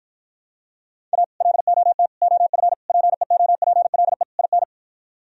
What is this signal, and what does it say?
Morse code, heard as a keyed tone, about 53 wpm, 700 Hz. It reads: A LOT OF PEOPLE IN